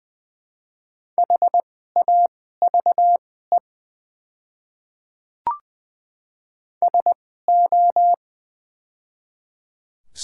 Morse code, heard as a keyed tone, about 20 wpm, spaced as for 15 wpm, 700 Hz. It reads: HAVE SO
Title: 2x Character Spacing - Top 100 Words - 20wpm